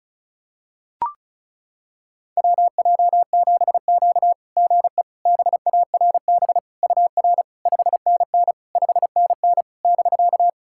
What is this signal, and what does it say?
Morse code, heard as a keyed tone, about 36 wpm, 690 Hz. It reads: WJ7Q GE BARB UR 5NN 5NN <BK>
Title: SOTA QSO Practice 35wpm